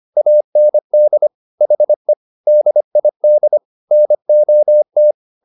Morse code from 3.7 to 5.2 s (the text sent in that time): NOT